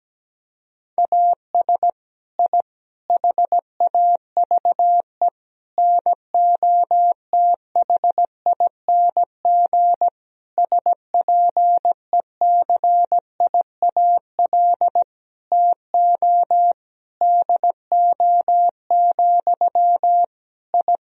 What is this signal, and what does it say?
Morse code, heard as a keyed tone, about 17 wpm, 700 Hz. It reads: AS I HAVE NOTHING SPECIAL TO DO, I
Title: Sentences from Top 1000 Words - Review Part 3 - 17wpm